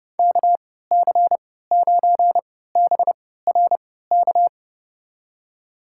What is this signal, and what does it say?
Morse code, heard as a keyed tone, about 32 wpm, spaced as for 17 wpm, 700 Hz. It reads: KC9BRK
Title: USA Call Signs - 6 Characters - 3x Character Spacing - 30wpm